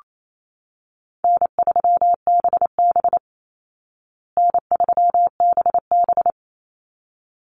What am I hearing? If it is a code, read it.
Morse code, heard as a keyed tone, about 28 wpm, 700 Hz. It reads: N3BB N3BB
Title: Call Signs Repeated 3x - 28wpm